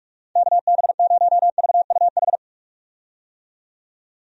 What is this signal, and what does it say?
Morse code, heard as a keyed tone, about 46 wpm, 700 Hz. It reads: KB0VUH
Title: USA Call Signs - 6 Characters - 45wpm